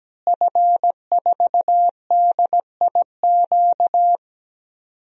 Morse code, heard as a keyed tone, about 17 wpm, 700 Hz. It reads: F4DIQ
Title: DXCC-100 Call Signs - 5 Characters - Repeated 3x - 17wpm